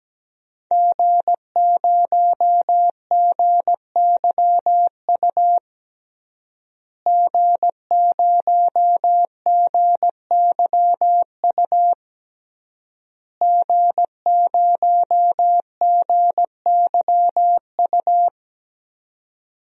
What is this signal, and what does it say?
Morse code, heard as a keyed tone, about 17 wpm, 700 Hz. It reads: G0GYU G0GYU G0GYU
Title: DXCC-100 Call Signs - 5 Characters - Repeated 3x - 17wpm